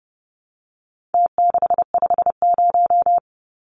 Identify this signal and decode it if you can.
Morse code, heard as a keyed tone, about 28 wpm, 700 Hz. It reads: T650
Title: Numbers - Contest Style - 4-Digit 30wpm